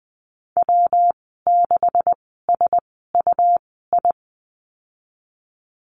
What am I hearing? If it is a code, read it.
Morse code, heard as a keyed tone, about 20 wpm, spaced as for 15 wpm, 700 Hz. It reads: W6SUI
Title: 128 Lesson 21g - Number 6 - Call Signs 20wpm